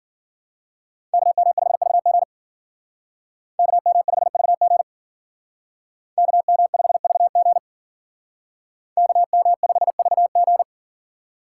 Morse code, heard as a keyed tone, about 52 wpm, 700 Hz. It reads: KM5VG KM5VG KM5VG KM5VG